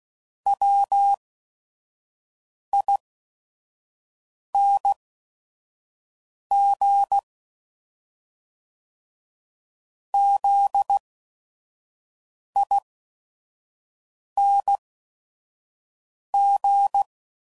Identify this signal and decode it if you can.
Morse code, heard as a keyed tone, about 16 wpm, spaced as for 5 wpm, 800 Hz. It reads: WING ZING
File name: Suffixes / Special ING's / VerySlowToFaster ING's